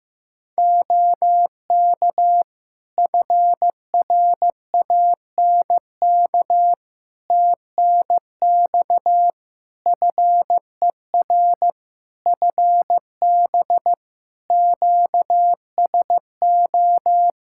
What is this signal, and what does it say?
Morse code, heard as a keyed tone, about 15 wpm, 700 Hz. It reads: OK FRANK TNX FER FB QSO
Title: Practice QSO - Exchange 4 - Ending - 15wpm